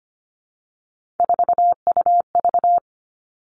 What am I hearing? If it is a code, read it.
Morse code, heard as a keyed tone, about 25 wpm, 700 Hz. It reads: 4UV
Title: Three Challenging Letter-Number Combinations 25wpm